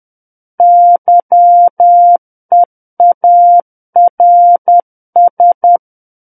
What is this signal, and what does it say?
Morse code, heard as a keyed tone, about 10 wpm, 700 Hz. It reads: YEARS
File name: Book One / 01-Chapter 1 - 10 WPM